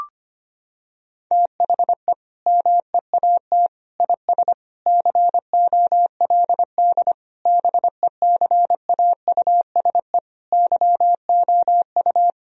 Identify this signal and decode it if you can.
Morse code, heard as a keyed tone, about 25 wpm, 700 Hz. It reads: THE MEAT IS COLD BECAUSE YOU
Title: Sentences from Top 1000 Words - Review Part 3 - 25wpm